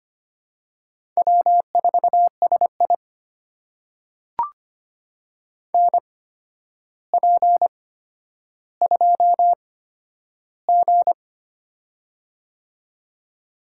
Morse code, heard as a keyed tone, about 25 wpm, spaced as for 11 wpm, 700 Hz. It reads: W4SI NP2G
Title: USA Call Signs - 4 Characters - 8x Character Spacing - 25wpm